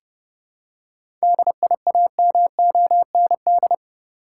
Morse code, heard as a keyed tone, about 30 wpm, 700 Hz. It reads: DIAMOND
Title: Seven Letter Words - 30wpm